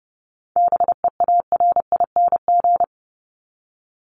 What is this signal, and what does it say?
Morse code, heard as a keyed tone, about 30 wpm, 700 Hz. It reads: BEARING
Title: ING Suffix - Top 100 Words with Base Words 30wpm